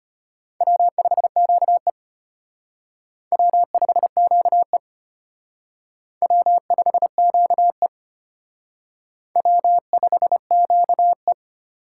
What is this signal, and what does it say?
Morse code, heard as a keyed tone, about 32 wpm, 700 Hz. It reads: W5QE W5QE W5QE W5QE